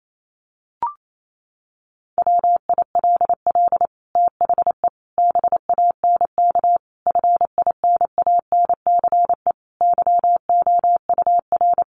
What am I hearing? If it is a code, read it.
Morse code, heard as a keyed tone, about 28 wpm, 700 Hz. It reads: WILL THE BANK FINANCE YOUR
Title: Sentences from Top 1500 Words 28wpm